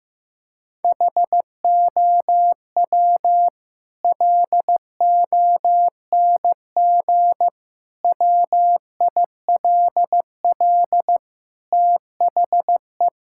Morse code, heard as a keyed tone, about 16 wpm, 700 Hz. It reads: HOW LONG WILL THE
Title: Sentences from Top 1000 Words - Review Part 6 - 15wpm